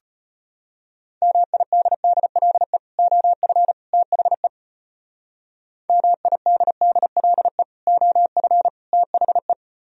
Morse code, heard as a keyed tone, about 32 wpm, 700 Hz. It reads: MIDDLE OF THE MIDDLE OF THE